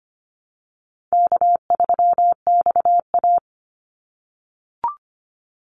Morse code, heard as a keyed tone, about 25 wpm, 700 Hz. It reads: K3XA
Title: Call Signs - 25wpm